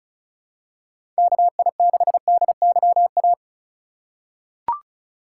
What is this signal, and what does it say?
Morse code, heard as a keyed tone, about 34 wpm, 700 Hz. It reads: KI6DYA
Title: USA Call Signs - 2x Character Spacing - 35wpm